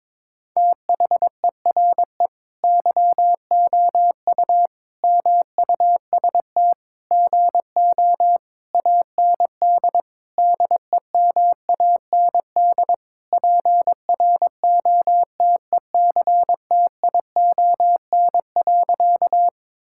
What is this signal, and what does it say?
Morse code, heard as a keyed tone, about 22 wpm, 700 Hz. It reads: THERE YOU MUST GO AND DEMAND PROTECTION.
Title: Sentences from Top 1600 Words 22wpm